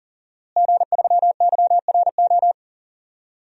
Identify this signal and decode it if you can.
Morse code, heard as a keyed tone, about 40 wpm, 700 Hz. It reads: G3YRO
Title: DXCC-100 Call Signs - 5 Characters - Repeated 3x - 40wpm